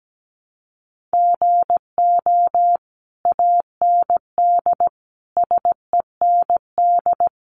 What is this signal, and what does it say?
Morse code, heard as a keyed tone, about 17 wpm, 700 Hz. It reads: GO AND SEND
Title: Sentences from Top 1500 Words 17wpm